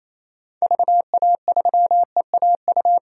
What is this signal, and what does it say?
Morse code, heard as a keyed tone, about 28 wpm, 700 Hz. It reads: VA3EAU
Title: DXCC-100 Call Signs - 6 Characters - 28wpm